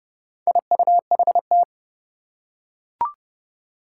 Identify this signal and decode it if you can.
Morse code, heard as a keyed tone, about 30 wpm, 700 Hz. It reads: IUHT
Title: Four Challenging Letter-Number Combinations 30wpm